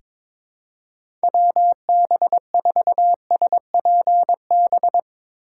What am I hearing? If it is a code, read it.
Morse code, heard as a keyed tone, about 22 wpm, 700 Hz. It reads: WB4SPB
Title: K1USN SST QSO Practice - 22wpm